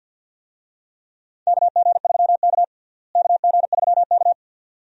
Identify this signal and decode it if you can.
Morse code, heard as a keyed tone, about 47 wpm, 700 Hz. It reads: KG3K KG3K
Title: USA Call Signs - 4 Characters - Repeated 2x - 50wpm